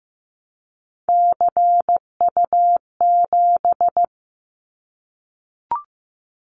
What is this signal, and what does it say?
Morse code, heard as a keyed tone, about 15 wpm, 700 Hz. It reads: CU7
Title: Callsign Prefixes for Top 50 DXCC - 15wpm